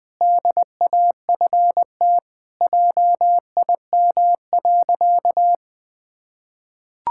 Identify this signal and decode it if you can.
Morse code, heard as a keyed tone, about 20 wpm, 700 Hz. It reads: DAFT JIM.
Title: Pangram Challenge v2 - 20wpm